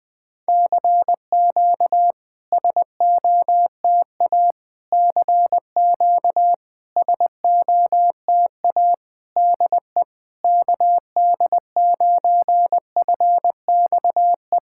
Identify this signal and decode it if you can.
Morse code, heard as a keyed tone, about 20 wpm, 700 Hz. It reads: CQ SOTA CQ SOTA DE KD9FXE
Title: SOTA QSO Practice 20wpm